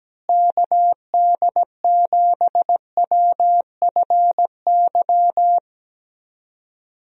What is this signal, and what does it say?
Morse code, heard as a keyed tone, about 17 wpm, 700 Hz. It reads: KD7WFY
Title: USA Call Signs - 6 Characters - Repeated 2x - 17wpm